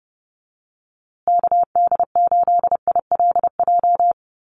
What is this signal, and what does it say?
Morse code, heard as a keyed tone, about 30 wpm, 700 Hz. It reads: KD8ILJ
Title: USA Call Signs Repeated 3x 30wpm